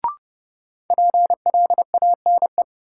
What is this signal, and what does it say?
Morse code, heard as a keyed tone, about 29 wpm, 700 Hz. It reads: PLANE
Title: Sentences from Top 1000 Words - Review Part 3 - New Word Drill - 30wpm